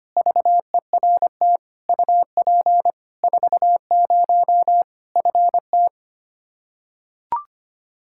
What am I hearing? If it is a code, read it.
Morse code, heard as a keyed tone, about 25 wpm, 700 Hz. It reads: VERT UP 40 FT
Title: K1USN SST QSO Practice 25wpm